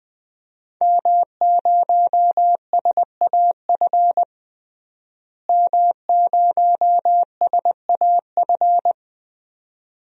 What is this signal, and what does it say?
Morse code, heard as a keyed tone, about 20 wpm, 700 Hz. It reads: M0SAF M0SAF
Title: DXCC-100 Call Signs - 5 Characters - Repeated 2x - 20wpm